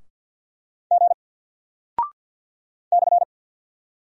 Morse code, heard as a keyed tone, about 50 wpm, 700 Hz. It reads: G /
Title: Single Character (Letter:Number:Punctuation:Prosign) 50wpm